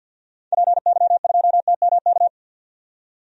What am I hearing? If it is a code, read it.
Morse code, heard as a keyed tone, about 46 wpm, 700 Hz. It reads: PY2TMK
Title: DXCC-100 Call Signs - 6 Characters - 50wpm